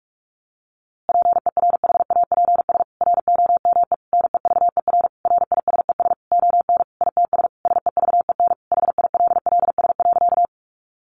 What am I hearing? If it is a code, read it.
Morse code, heard as a keyed tone, about 44 wpm, 700 Hz. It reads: PERHAPS ROME NEVER RISES ON ITS SEVEN HILLS.
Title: A Sound Of Thunder 45wpm